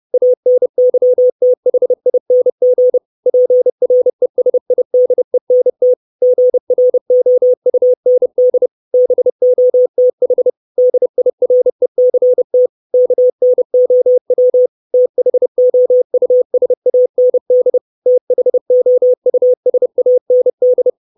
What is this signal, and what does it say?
Morse code, heard as a keyed tone, about 30 wpm, 500 Hz. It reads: ANYTHING PRESIDENT GROUND BOTH DIRECT KNOW THOUSAND THOUSAND
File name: 632 Words 30WPM